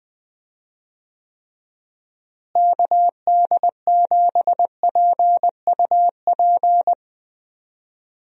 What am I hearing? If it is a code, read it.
Morse code, heard as a keyed tone, about 20 wpm, 700 Hz. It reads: KD7PUP
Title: USA Call Signs - 6 Characters - Repeated 2x - 20wpm